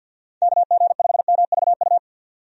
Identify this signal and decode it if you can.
Morse code, heard as a keyed tone, about 50 wpm, 700 Hz. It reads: KG5MVU
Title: USA Call Signs - 6 Characters - 50wpm